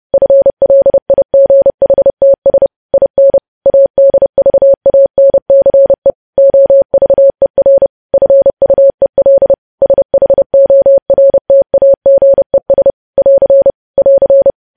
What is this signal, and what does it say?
Morse code, heard as a keyed tone, about 30 wpm, 550 Hz. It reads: FLIGHTS IN ADVANCE OVER FUEL SHORTAGES <AR> <AR>